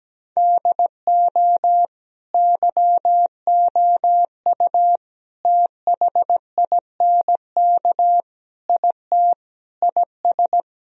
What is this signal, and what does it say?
Morse code, heard as a keyed tone, about 17 wpm, 700 Hz. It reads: DO YOU THINK IT IS
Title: Sentences from Top 1000 Words - Review Part 1 - 17wpm